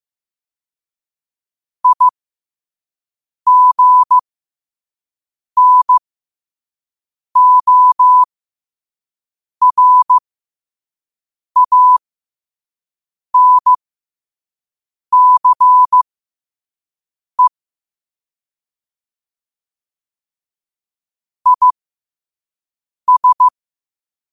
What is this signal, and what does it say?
Morse code, heard as a keyed tone, about 15 wpm, spaced as for 5 wpm, 1000 Hz. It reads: IGNORANCE IS